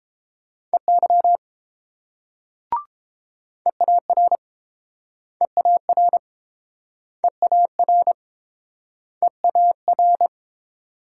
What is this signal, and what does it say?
Morse code, heard as a keyed tone, about 30 wpm, 700 Hz. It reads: EY EAR EAR EAR EAR